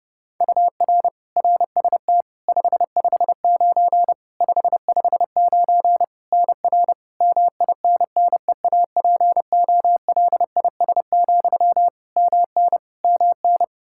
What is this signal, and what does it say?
Morse code, heard as a keyed tone, about 32 wpm, 700 Hz. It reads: UR RST 559 559 NR MINNEAPOLIS, MN MN
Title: Practice QSO - Exchange 1 - Beginning - 30wpm